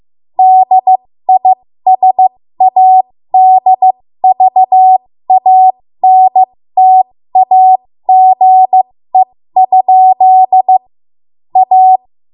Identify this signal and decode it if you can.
Morse code, heard as a keyed tone, about 15 wpm, spaced as for 13 wpm, 750 Hz. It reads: DISADVANTAGE? A
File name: Morse / 13WPM